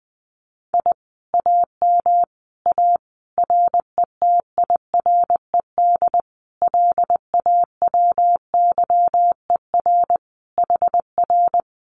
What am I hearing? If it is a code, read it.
Morse code, heard as a keyed tone, about 20 wpm, 700 Hz. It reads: I AM A RETIRED LAWYER HR
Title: Common QSO Phrases Encore 20wpm